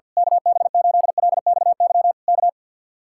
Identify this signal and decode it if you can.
Morse code, heard as a keyed tone, about 50 wpm, 700 Hz. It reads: KB8LXY K
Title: Speed-Racing Common QSO Phrases - 50wpm